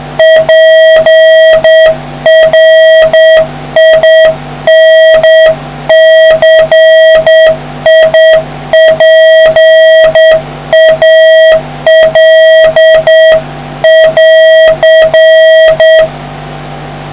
Morse code, heard as a keyed tone, about 9 wpm, 1900 Hz, with noise. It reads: PRINCIPAL<AR>